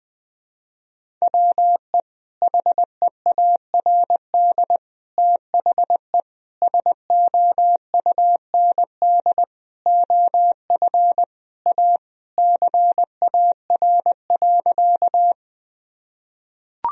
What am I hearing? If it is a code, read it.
Morse code, heard as a keyed tone, about 21 wpm, 700 Hz. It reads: WE HEARD THE SOUND OF A CAR.
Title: Sentences from Top 1000 Words - Review Part 1 - 20wpm